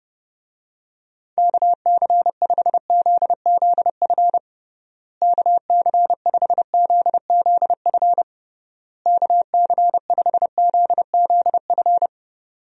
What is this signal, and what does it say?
Morse code, heard as a keyed tone, about 30 wpm, 700 Hz. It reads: KC5ZZF KC5ZZF KC5ZZF
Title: USA Call Signs - 6 Characters - Repeated 3x - 30wpm